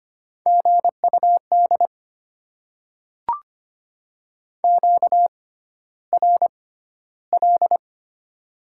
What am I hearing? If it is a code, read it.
Morse code, heard as a keyed tone, about 25 wpm, spaced as for 13 wpm, 700 Hz. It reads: GUD QRL
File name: Top 20 QSO Elements - 6x Character Spacing - 25wpm